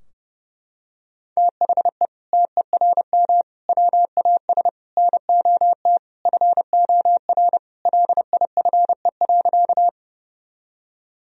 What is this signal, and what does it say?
Morse code, heard as a keyed tone, about 28 wpm, 700 Hz. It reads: THE TERM WAS NOT FOR LIFE.
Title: Sentences from Top 1000 Words - Review Part 4 - 30wpm